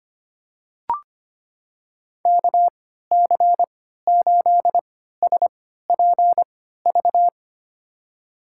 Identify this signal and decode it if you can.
Morse code, heard as a keyed tone, about 25 wpm, spaced as for 14 wpm, 700 Hz. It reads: KC8SPV
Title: USA Call Signs - 3x Character Spacing - 25wpm